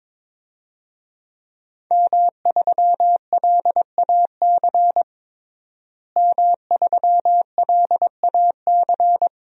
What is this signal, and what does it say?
Morse code, heard as a keyed tone, about 22 wpm, 700 Hz. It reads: M3LAC M3LAC
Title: DXCC-100 Call Signs - 5 Characters - Repeated 3x - 22wpm